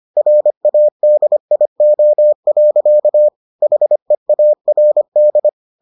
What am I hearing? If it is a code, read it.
Morse code, heard as a keyed tone, about 25 wpm, 600 Hz. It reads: RADIO. HEARD